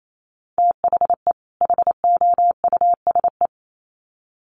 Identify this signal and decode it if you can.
Morse code, heard as a keyed tone, about 28 wpm, 700 Hz. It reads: THE HOUSE
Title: Sets of 2 Words 28wpm